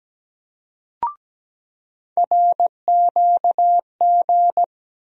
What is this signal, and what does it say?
Morse code, heard as a keyed tone, about 17 wpm, 700 Hz. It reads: RQG
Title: USA Call Sign Postfixes - 3 Letters 17wpm